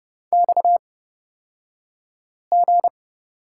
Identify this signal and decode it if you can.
Morse code, heard as a keyed tone, about 30 wpm, spaced as for 5 wpm, 700 Hz. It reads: XG
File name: Lesson 37d - Letter x - Two Characters 30wpm